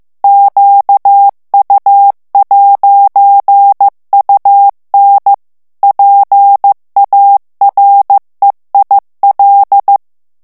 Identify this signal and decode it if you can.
Morse code, heard as a keyed tone, about 15 wpm, 800 Hz. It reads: QU'UN PAREIL
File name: FRANCAIS TEXTE CLAIR 2 VITESSE 14 MOTS MINUTES